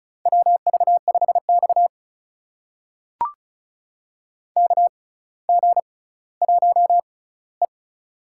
Morse code, heard as a keyed tone, about 36 wpm, spaced as for 18 wpm, 700 Hz. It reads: WV5X KG1E
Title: USA Call Signs - 4 Characters - 6x Character Spacing - 35wpm